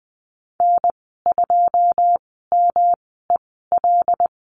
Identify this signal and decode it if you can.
Morse code, heard as a keyed tone, about 20 wpm, spaced as for 15 wpm, 700 Hz. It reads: N2MEL
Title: 167 Lesson 27g - Number 7 - Call Signs 20wpm